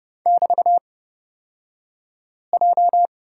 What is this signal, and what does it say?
Morse code, heard as a keyed tone, about 29 wpm, spaced as for 5 wpm, 700 Hz. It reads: <BT>J
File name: Lesson 38d - Letter q - Two Characters 30wpm